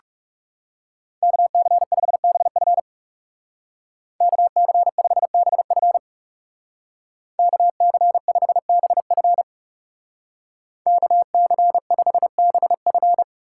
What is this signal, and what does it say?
Morse code, heard as a keyed tone, about 36 wpm, 700 Hz. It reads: KC5BF KC5BF KC5BF KC5BF
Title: Speed Racing - USA 2x2 Call Signs 30wpm